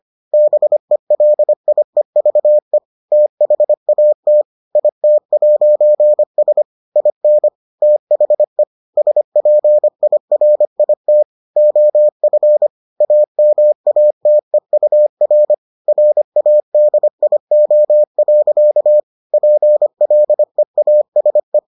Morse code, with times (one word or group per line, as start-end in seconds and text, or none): 0.3-2.8 BELIEVE
3.1-4.4 THAT
4.8-6.6 IT'S
7.0-7.5 IN
7.8-8.6 THE
9.0-11.2 SPIRIT
11.6-12.7 OF
13.0-15.5 AMATEUR
15.9-19.0 RADIO.
19.3-21.6 PLEASE